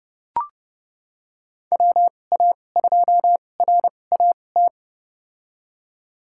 Morse code, heard as a keyed tone, about 32 wpm, spaced as for 22 wpm, 700 Hz. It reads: WA2RAT
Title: USA Call Signs - 6 Characters - 2x Character Spacing - 30wpm